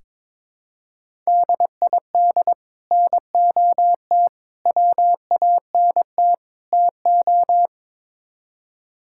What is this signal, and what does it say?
Morse code, heard as a keyed tone, about 22 wpm, 700 Hz. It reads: DID NOT WANT TO
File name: practice / 22wpm / Build A Phrase - 5-Word Phrases - 22wpm